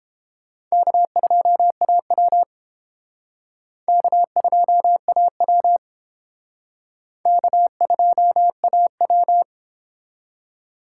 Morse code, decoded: K2AW K2AW K2AW

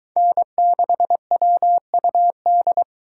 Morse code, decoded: N6WUD

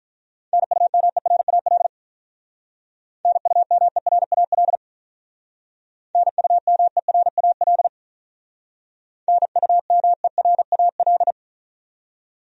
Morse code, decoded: NUMERAL NUMERAL NUMERAL NUMERAL

